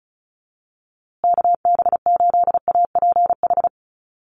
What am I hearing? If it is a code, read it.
Morse code, heard as a keyed tone, about 34 wpm, 700 Hz. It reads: KB8APH
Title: USA Call Signs Repeated 3x 35wpm